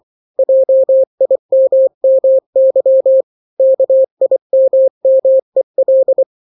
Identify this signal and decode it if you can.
Morse code, heard as a keyed tone, about 24 wpm, 530 Hz. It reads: JIMMY KIMMEL
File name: daily / 23wpm